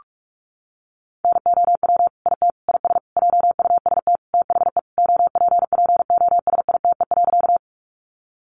Morse code, heard as a keyed tone, about 44 wpm, 700 Hz. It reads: NOW IT IS JUST THE OPPOSITE.
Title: Sentences from Top 1300 Words 45wpm